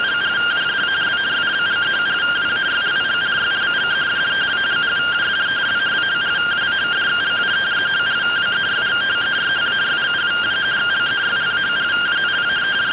Two-tone teleprinter signal, about 170 Hz shift.